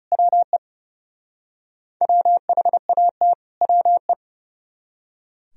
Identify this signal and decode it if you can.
Morse code, heard as a keyed tone, about 37 wpm, 700 Hz. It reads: WE WHAT WE
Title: Speed-Racing - Sets of 2 Words 30wpm